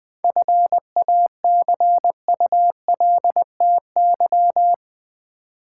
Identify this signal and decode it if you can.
Morse code, heard as a keyed tone, about 20 wpm, 700 Hz. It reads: FACULTY
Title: 7 Letter Words - First 3 Letters Given - 20wpm